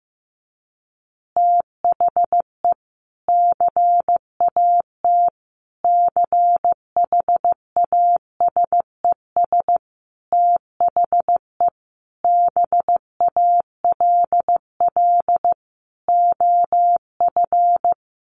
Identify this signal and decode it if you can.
Morse code, heard as a keyed tone, about 14 wpm, 700 Hz. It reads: THE CAT CHASES THE BALL OF
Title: Additional Easy Phrases 15wpm